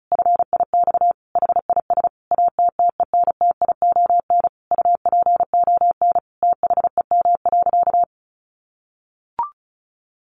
Morse code, heard as a keyed tone, about 34 wpm, 700 Hz. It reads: FIX HIS ATTENTION UPON THEM.
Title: Sentences from Top 1400 Words 35wpm